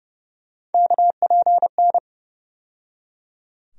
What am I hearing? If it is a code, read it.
Morse code, heard as a keyed tone, about 30 wpm, 700 Hz. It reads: KPN
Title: USA Call Sign Postfixes - 3 Letters 30wpm